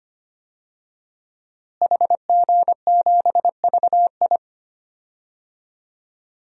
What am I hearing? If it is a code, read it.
Morse code, heard as a keyed tone, about 25 wpm, 700 Hz. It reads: HG7VI
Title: DXCC-100 Call Signs - 5 Characters - Repeated 3x - 25wpm